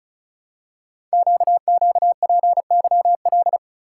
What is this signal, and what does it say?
Morse code, heard as a keyed tone, about 36 wpm, 700 Hz. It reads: QQPYL
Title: Code Groups - 5 Letters - 35wpm